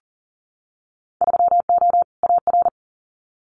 Morse code, heard as a keyed tone, about 41 wpm, 700 Hz. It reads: 3O AR
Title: Winter Field Day Exchanges Encore 40wpm